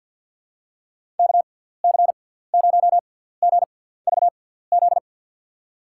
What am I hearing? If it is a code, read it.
Morse code, heard as a keyed tone, about 49 wpm, spaced as for 17 wpm, 700 Hz. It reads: KC0GVZ
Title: USA Call Signs - 6 Characters - 6x Character Spacing - 50wpm